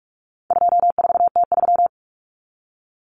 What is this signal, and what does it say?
Morse code, heard as a keyed tone, about 44 wpm, 700 Hz. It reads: 24T3
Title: Common Cut Numbers - 4-Digit 45wpm